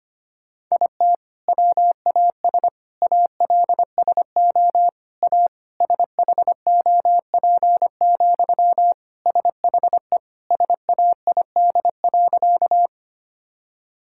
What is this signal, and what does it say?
Morse code, heard as a keyed tone, about 25 wpm, 700 Hz. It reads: IT WAS ALSO A SHOP, SHE SAID.
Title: Sentences from Top 1600 Words 25wpm